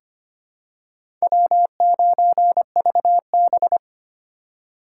Morse code, heard as a keyed tone, about 25 wpm, 700 Hz. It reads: W9VB
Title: USA Call Signs - 4 Characters - 2x Character Spacing - 25wpm